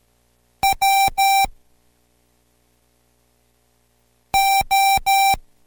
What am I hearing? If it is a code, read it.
Morse code, heard as a keyed tone, about 13 wpm, spaced as for 3 wpm, 780 Hz. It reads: WO